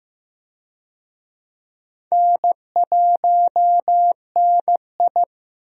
Morse code, heard as a keyed tone, about 15 wpm, 700 Hz. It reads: N1NI
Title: USA Call Signs - 4 Characters - 15wpm